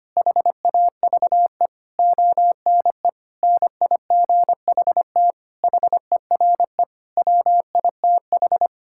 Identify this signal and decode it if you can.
Morse code, heard as a keyed tone, about 25 wpm, 700 Hz. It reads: HAVE ONE NIGHT HERE WITH
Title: Sentences from Top 1600 Words 25wpm